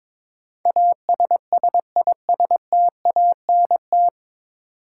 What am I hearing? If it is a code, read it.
Morse code, heard as a keyed tone, about 22 wpm, 700 Hz. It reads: ASSISTANT
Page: Morse Code Ninja practice 22wpm